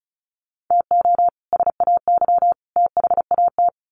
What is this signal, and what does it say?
Morse code, heard as a keyed tone, about 34 wpm, 700 Hz. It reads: TO SAY THAT
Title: Sets of 3 Words 35wpm